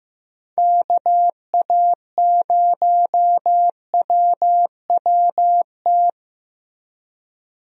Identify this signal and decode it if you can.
Morse code, heard as a keyed tone, about 15 wpm, 700 Hz. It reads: KA0WWT
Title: USA Call Signs - 6 Characters - Repeated 3x - 15wpm